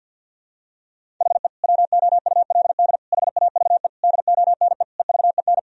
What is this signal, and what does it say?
Morse code, heard as a keyed tone, about 52 wpm, 700 Hz. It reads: HE WOULD HAVE DONE EVEN
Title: Sentences from Top 1000 Words - Review Part 1 - 50wpm